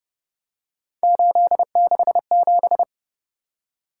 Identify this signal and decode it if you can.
Morse code, heard as a keyed tone, about 31 wpm, 700 Hz. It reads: 867